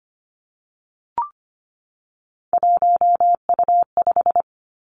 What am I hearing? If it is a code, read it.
Morse code, heard as a keyed tone, about 25 wpm, 700 Hz. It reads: 1U5
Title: Three Challenging Letter-Number Combinations 25wpm